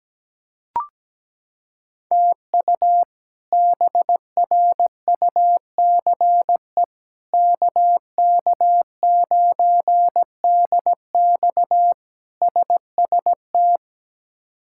Morse code, heard as a keyed tone, about 17 wpm, 700 Hz. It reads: TU BRUCE KK9DX SST
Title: K1USN SST QSO Practice - 17wpm